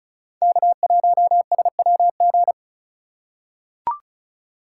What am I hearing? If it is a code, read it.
Morse code, heard as a keyed tone, about 36 wpm, 700 Hz. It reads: K1SWG
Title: USA Call Signs - 5 Characters - 2x Character Spacing - 35wpm